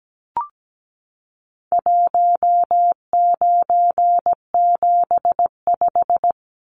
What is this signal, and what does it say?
Morse code, heard as a keyed tone, about 17 wpm, 700 Hz. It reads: 1975
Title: Numbers - 4-Digit 17wpm